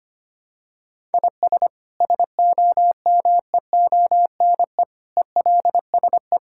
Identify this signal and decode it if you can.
Morse code, heard as a keyed tone, about 25 wpm, 700 Hz. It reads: IS SOMEONE ELSE
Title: Sentences from Top 1000 Words - Review Part 5 - 25wpm